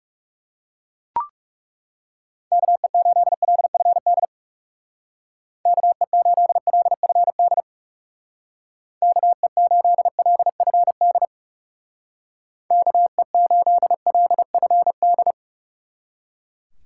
Morse code, heard as a keyed tone, about 36 wpm, 700 Hz. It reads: KE8LFD KE8LFD KE8LFD KE8LFD
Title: Speed Racing - USA Call Signs 30wpm